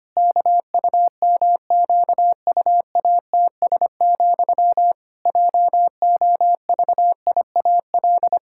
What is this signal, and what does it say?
Morse code, heard as a keyed tone, about 25 wpm, 700 Hz. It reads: KUMQUATS, JOVIAL
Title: Pangram Challenge v2 - 25wpm